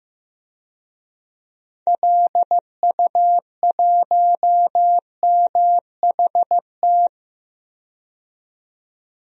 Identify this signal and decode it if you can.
Morse code, heard as a keyed tone, about 15 wpm, 700 Hz. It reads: LU1MHT
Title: DXCC-100 Call Signs - 6 Characters - 15wpm